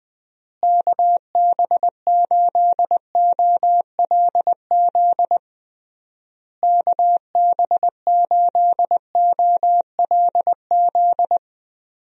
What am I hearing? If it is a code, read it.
Morse code, heard as a keyed tone, about 20 wpm, 700 Hz. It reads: KB8OLZ KB8OLZ
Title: USA Call Signs - 6 Characters - Repeated 3x - 20wpm